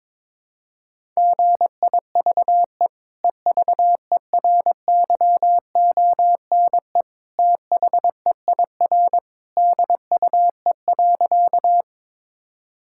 Morse code, heard as a keyed tone, about 22 wpm, 700 Hz. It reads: GIVE EVERYONE THEIR DUE.
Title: Sentences from Top 1000 Words - Review Part 6 - 22wpm